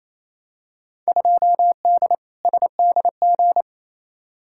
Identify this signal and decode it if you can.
Morse code, heard as a keyed tone, about 28 wpm, 700 Hz. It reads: 2D SDG
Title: Field Day Exchanges Encore - 28wpm